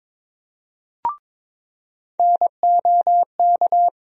Morse code, heard as a keyed tone, about 21 wpm, 700 Hz. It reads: NOK
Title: USA Call Sign Postfixes - 3 Letters 22wpm